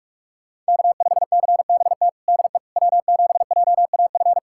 Morse code, heard as a keyed tone, about 46 wpm, 700 Hz. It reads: K5CBT DE W7JAF
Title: USA Callsign DE Callsign - 5 Characters - 45wpm